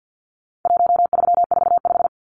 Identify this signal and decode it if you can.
Morse code, heard as a keyed tone, about 50 wpm, 700 Hz. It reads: 1345
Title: Numbers - 4-Digit 50wpm